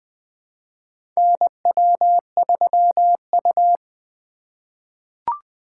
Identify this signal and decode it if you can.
Morse code, heard as a keyed tone, about 20 wpm, 700 Hz. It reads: NW3U
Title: USA Call Signs - 4 Characters - 2x Character Spacing - 20wpm